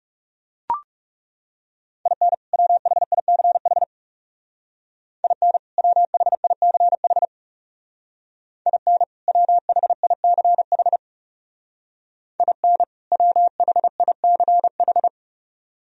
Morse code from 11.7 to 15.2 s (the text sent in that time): IN WHICH